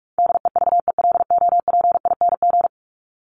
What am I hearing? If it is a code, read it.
Morse code, heard as a keyed tone, about 44 wpm, 700 Hz. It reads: DEVELOPING
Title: ING Suffix - Top 100 Words with Base Words 45wpm